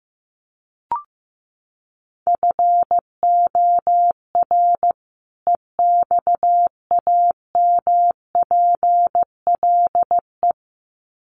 Morse code, heard as a keyed tone, about 15 wpm, 700 Hz. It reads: FOR EXAMPLE
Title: Sets of 2 Words 15wpm